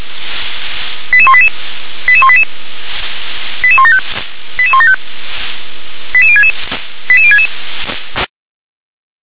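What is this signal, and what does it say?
Five-tone selective call (ZVEI), heard as quick runs of five tones, tones of about 70 ms. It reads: 88188 88188 88175 88175 88066 88066
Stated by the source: ZVEI five-tone call 88105, received by SDR.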